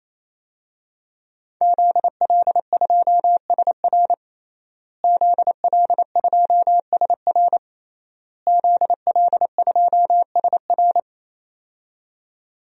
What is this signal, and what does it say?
Morse code, heard as a keyed tone, about 28 wpm, 700 Hz. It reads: ZL2SR ZL2SR ZL2SR
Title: DXCC-100 Call Signs - 5 Characters - Repeated 3x - 28wpm